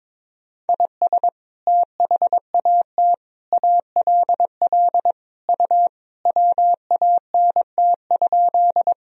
Morse code, heard as a keyed tone, about 22 wpm, 700 Hz. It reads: IS THAT ALL U WANT?
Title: Sentences from Top 200 Words and Abbreviations 22wpm